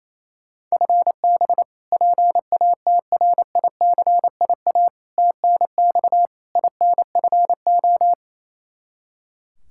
Morse code, heard as a keyed tone, about 28 wpm, 700 Hz. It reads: FB PATRICIA TNX INFO
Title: Practice QSO - Exchange 3 - More Info - 28wpm